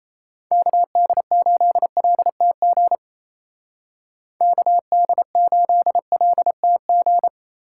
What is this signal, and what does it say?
Morse code, heard as a keyed tone, about 36 wpm, 700 Hz. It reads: KD8LTG KD8LTG